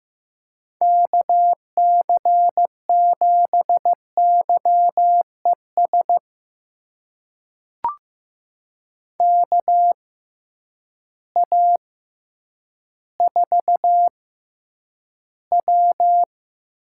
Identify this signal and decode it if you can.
Morse code, heard as a keyed tone, about 15 wpm, spaced as for 9 wpm, 700 Hz. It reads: KC7YES KA4W
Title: USA Call Signs - 6 Characters - 6x Character Spacing - 15wpm